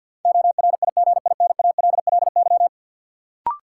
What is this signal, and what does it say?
Morse code, heard as a keyed tone, about 52 wpm, 700 Hz. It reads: ORIGINALLY
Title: Sentences from Top 1500 Words - New Word Drill Alt - 50wpm